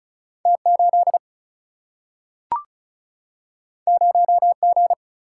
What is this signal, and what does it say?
Morse code, heard as a keyed tone, about 36 wpm, 700 Hz. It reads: T8 0G
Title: Single-Letter Single-Number 35wpm